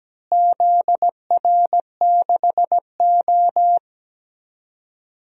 Morse code, heard as a keyed tone, about 17 wpm, 700 Hz. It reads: ZR6O
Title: DXCC-100 Call Signs - 4 Characters - Repeated 2x - 17wpm